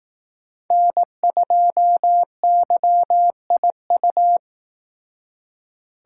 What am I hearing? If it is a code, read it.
Morse code, heard as a keyed tone, about 18 wpm, 700 Hz. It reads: N2YIU
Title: Speed Racing - USA 1x3 Call Signs 15wpm